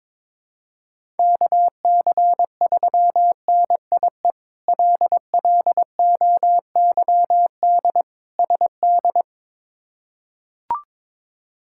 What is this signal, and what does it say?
Morse code, heard as a keyed tone, about 22 wpm, 700 Hz. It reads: KC3NIE LLOYD SD
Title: K1USN SST QSO Practice - 22wpm